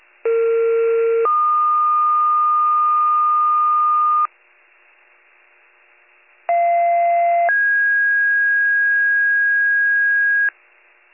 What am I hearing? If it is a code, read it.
Two-tone page, heard as a one-second tone, then a three-second tone, about 460 then 1220 Hz, about 690 then 1650 Hz. It is heard from 0.2 to 10.5 s.